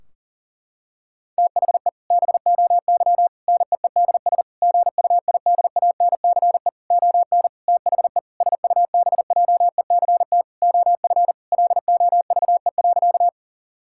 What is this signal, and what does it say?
Morse code, heard as a keyed tone, about 37 wpm, 700 Hz. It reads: THE BOY NEEDS GUIDANCE ON THE SUBJECT OF LOVE.
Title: Sentences from Top 1600 Words 40wpm